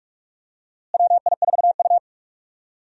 Morse code, heard as a keyed tone, about 46 wpm, 700 Hz. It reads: WI4U
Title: USA Call Signs - 4 Characters - Repeated 3x - 45wpm